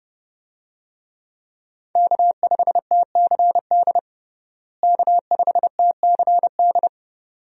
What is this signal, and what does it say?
Morse code, heard as a keyed tone, about 32 wpm, 700 Hz. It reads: K5TCD K5TCD
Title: USA Call Signs - 5 Characters - Repeated 2x - 30wpm